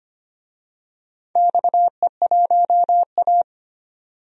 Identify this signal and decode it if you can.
Morse code, heard as a keyed tone, about 25 wpm, 700 Hz. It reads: XE1A
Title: DXCC-100 Call Signs - 4 Characters - 25wpm